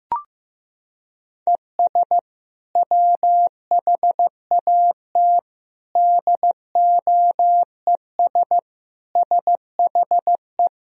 Morse code, heard as a keyed tone, about 15 wpm, 700 Hz. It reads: ES WHAT DOES SHE
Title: Sentences from Top 400 Words and Abbreviations 15wpm